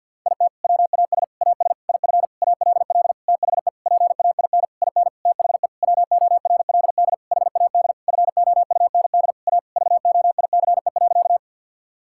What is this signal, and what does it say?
Morse code, heard as a keyed tone, about 50 wpm, 700 Hz. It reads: IT WAS AS IF ALL THE PAIN IN THE WORLD HAD FOUND A VOICE.
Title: Sentences from Top 1600 Words 50wpm